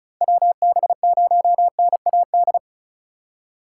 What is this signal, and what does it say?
Morse code, heard as a keyed tone, about 36 wpm, 700 Hz. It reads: WB0NAD
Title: USA Call Signs - 4x Character Spacing - 35wpm